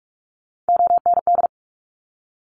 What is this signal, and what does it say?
Morse code, heard as a keyed tone, about 46 wpm, 700 Hz. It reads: OND
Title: Top Three Letter Combinations in Words 45wpm